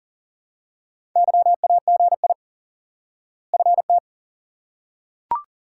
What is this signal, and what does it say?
Morse code, heard as a keyed tone, about 40 wpm, 700 Hz. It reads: YAGI FT
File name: IQE Course 3 - Lesson 89 - YAGI - Two Elements - 40wpm